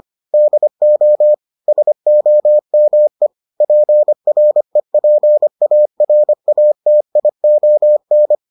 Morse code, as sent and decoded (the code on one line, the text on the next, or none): -.. --- / ... --- -- . / .--. .-. .--. .- .-. .- - .. --- -.
DO SOME PREPARATION